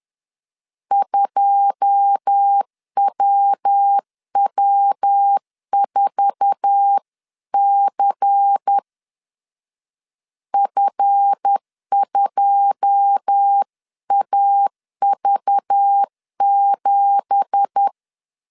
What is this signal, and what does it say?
Morse code, heard as a keyed tone, about 10 wpm, 800 Hz. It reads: 2WW4C F2AV7
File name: CW / DET / 1 9 1500C S B 10wpm 4439